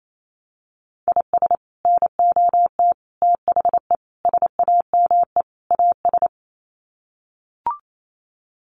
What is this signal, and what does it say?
Morse code, heard as a keyed tone, about 28 wpm, 700 Hz. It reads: IS NOT THE SAME AS
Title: Sets of 5 Words Encore 28wpm